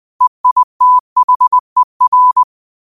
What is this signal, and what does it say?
Morse code, heard as a keyed tone, about 20 wpm, 1000 Hz. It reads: EITHER